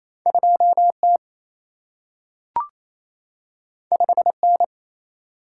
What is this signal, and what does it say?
Morse code, heard as a keyed tone, about 28 wpm, 700 Hz. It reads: 2T 5N
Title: Single-Letter Single-Number 28wpm